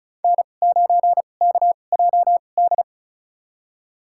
Morse code, heard as a keyed tone, about 36 wpm, spaced as for 25 wpm, 700 Hz. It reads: N9KJD
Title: USA Call Signs - 5 Characters - 2x Character Spacing - 35wpm